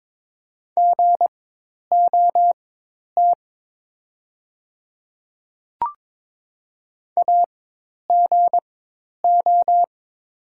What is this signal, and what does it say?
Morse code, heard as a keyed tone, about 22 wpm, spaced as for 10 wpm, 690 Hz. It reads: GOT AGO
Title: 4x Character Spacing - Top 500 Words - 22wpm